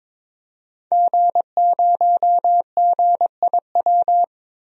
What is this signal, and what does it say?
Morse code, heard as a keyed tone, about 22 wpm, 700 Hz. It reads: G0GIW
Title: DXCC-100 Call Signs - 5 Characters - Repeated 3x - 22wpm